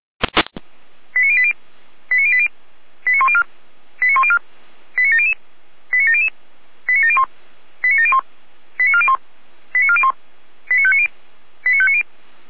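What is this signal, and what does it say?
Five-tone selective call (ZVEI), heard as quick runs of five tones, tones of about 70 ms. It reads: 89080 89080 89104 89104 89700 89700 89701 89701 89401 89401 89590 89590
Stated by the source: Received on an SDR, ZVEI five-tone selective call.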